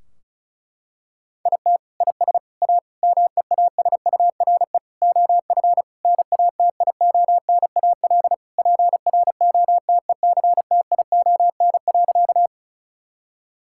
Morse code, heard as a keyed tone, about 36 wpm, 700 Hz. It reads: IT IS A MEASURE OF NATIONAL PROTECTION.
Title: Sentences from Top 1600 Words 35wpm